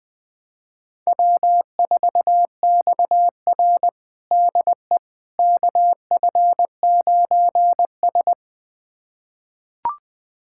Morse code, heard as a keyed tone, about 20 wpm, 700 Hz. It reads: W4XR DE KF9S
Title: USA Callsign DE Callsign - 4 Characters - 20wpm